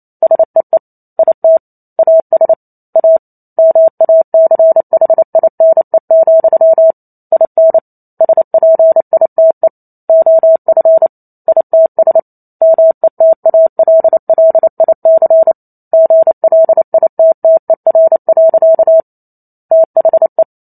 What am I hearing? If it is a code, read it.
Morse code, heard as a keyed tone, about 28 wpm, 650 Hz. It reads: SEE IT AS A MACHINE, IN SPITE OF ITS METALLIC GLITTER. THE